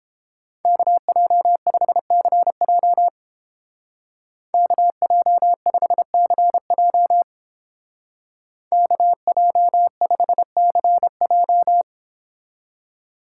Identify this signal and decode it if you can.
Morse code, heard as a keyed tone, about 29 wpm, 700 Hz. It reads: KJ5CJ KJ5CJ KJ5CJ